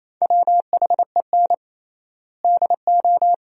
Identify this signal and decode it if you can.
Morse code, heard as a keyed tone, about 28 wpm, 700 Hz. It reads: WHEN DO
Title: 3x Word-Spacing - Sentences from Top 400 Words - 28wpm